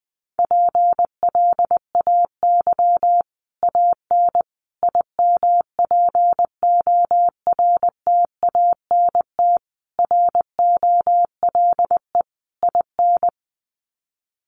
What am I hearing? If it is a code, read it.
Morse code, heard as a keyed tone, about 20 wpm, 700 Hz. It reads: PLAY AN IMPORTANT ROLE IN
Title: Sets of 5 Words Encore 20wpm